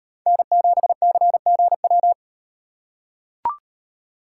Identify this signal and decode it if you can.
Morse code, heard as a keyed tone, about 38 wpm, 700 Hz. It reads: N7CGW